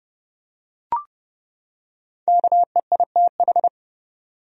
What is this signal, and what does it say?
Morse code, heard as a keyed tone, about 30 wpm, 700 Hz. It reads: KEITH
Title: USA Common Names - 30wpm